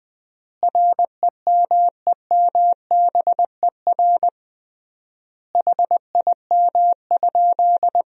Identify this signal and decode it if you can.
Morse code, heard as a keyed tone, about 20 wpm, 700 Hz. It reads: REMEMBER HIM?
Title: 3x Word-Spacing - Sentences from Top 400 Words - 20wpm